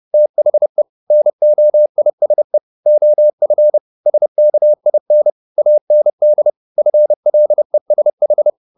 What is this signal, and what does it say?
Morse code, heard as a keyed tone, about 30 wpm, 600 Hz. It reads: THE NOISE OF SKIN AND FLESH